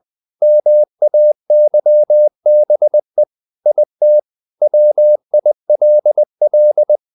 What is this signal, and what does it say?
Morse code, heard as a keyed tone, about 20 wpm, 600 Hz. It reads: MAYBE IT WILL
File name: the old man and the sea / the old man and the sea part 1 20wpm